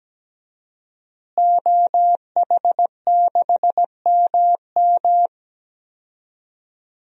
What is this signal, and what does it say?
Morse code, heard as a keyed tone, about 17 wpm, 700 Hz. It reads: OH6MM
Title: DXCC-100 Call Signs - 5 Characters - Repeated 3x - 17wpm